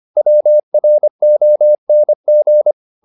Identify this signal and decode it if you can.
Morse code, heard as a keyed tone, about 25 wpm, 600 Hz. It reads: WRONG